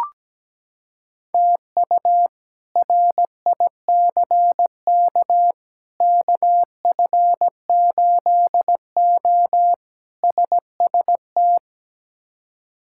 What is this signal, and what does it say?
Morse code, heard as a keyed tone, about 17 wpm, 700 Hz. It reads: TU RICK KF8O SST
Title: K1USN SST QSO Practice - 17wpm